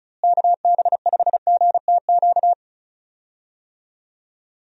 Morse code, heard as a keyed tone, about 34 wpm, 700 Hz. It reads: KB5GTQ
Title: USA Call Signs - 6 Characters - Repeated 2x - 35wpm